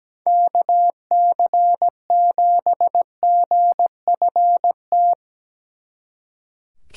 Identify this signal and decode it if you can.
Morse code, heard as a keyed tone, about 18 wpm, 700 Hz. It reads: KC7GFT
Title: Speed Racing - USA 2x3 Call Signs 17wpm